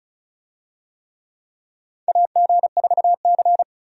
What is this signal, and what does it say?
Morse code, heard as a keyed tone, about 36 wpm, 700 Hz. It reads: AG4C
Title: USA Call Signs - 4 Characters - 35wpm